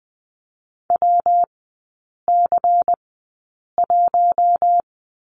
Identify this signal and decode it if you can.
Morse code, heard as a keyed tone, about 20 wpm, spaced as for 8 wpm, 700 Hz. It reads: WC1
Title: 121 Lesson 20e - Number 3 - Three Characters 20wpm